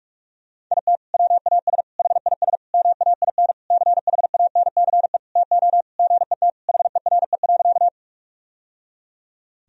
Morse code, heard as a keyed tone, about 46 wpm, 700 Hz. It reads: IT WAS HIS MAIN CHANCE TO GET HERE.
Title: Sentences from Top 1000 Words - Review Part 2 - 45wpm